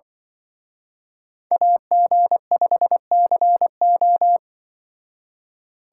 Morse code, heard as a keyed tone, about 24 wpm, 700 Hz. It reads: AG5CO
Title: Speed Racing - USA 2x2 Call Signs 20wpm